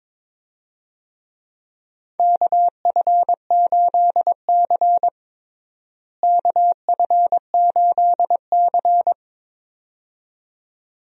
Morse code, heard as a keyed tone, about 22 wpm, 700 Hz. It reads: KF8C KF8C
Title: USA Call Signs - 4 Characters - Repeated 2x - 22wpm